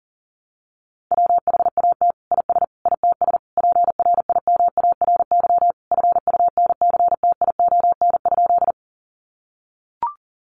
Morse code, heard as a keyed tone, about 40 wpm, 700 Hz. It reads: WHAT IS ITS PRIMARY FUNCTION?
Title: Sentences from Top 1100 Words 40wpm